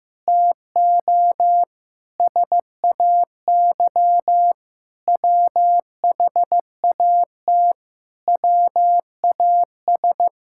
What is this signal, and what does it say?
Morse code, heard as a keyed tone, about 15 wpm, 700 Hz. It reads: TO SAY WHAT WAS
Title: Sentences from Top 1000 Words - Review Part 5 - 15wpm